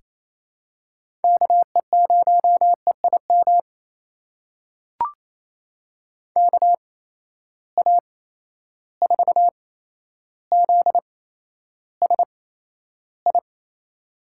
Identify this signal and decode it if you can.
Morse code, heard as a keyed tone, about 28 wpm, spaced as for 12 wpm, 700 Hz. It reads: KE0EIM KA4ZSI
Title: USA Call Signs - 6 Characters - 8x Character Spacing - 28wpm